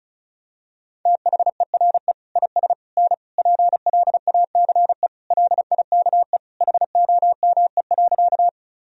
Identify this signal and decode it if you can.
Morse code, heard as a keyed tone, about 36 wpm, 700 Hz. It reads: THERE IS N PLACE LIKE HOME.
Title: Sentences from Top 200 Words and Abbreviations 35wpm